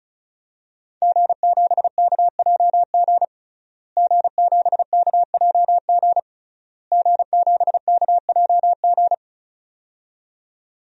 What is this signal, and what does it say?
Morse code, heard as a keyed tone, about 36 wpm, 700 Hz. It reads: G7KJG G7KJG G7KJG
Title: DXCC-100 Call Signs - 5 Characters - Repeated 3x - 35wpm